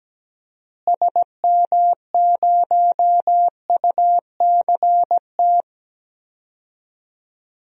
Morse code, heard as a keyed tone, about 17 wpm, 700 Hz. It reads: SM0UCT
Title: DXCC-100 Call Signs - 6 Characters - Repeated 3x - 17wpm